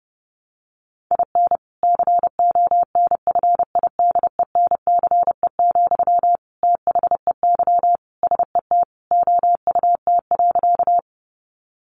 Morse code, heard as a keyed tone, about 28 wpm, 700 Hz. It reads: IN CONFIDENCE, THEY SET OUT.
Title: Sentences from Top 1500 Words 30wpm